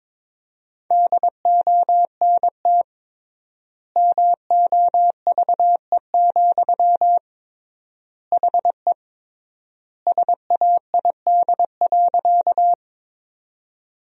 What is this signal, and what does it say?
Morse code, heard as a keyed tone, about 22 wpm, 700 Hz. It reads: DONT MOVE, HE SAID.
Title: 3x Word-Spacing - Sentences from Top 300 Words - 22wpm